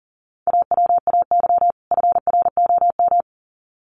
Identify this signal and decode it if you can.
Morse code, heard as a keyed tone, about 40 wpm, 700 Hz. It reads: AWAY FROM
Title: Sets of 2 Words 40wpm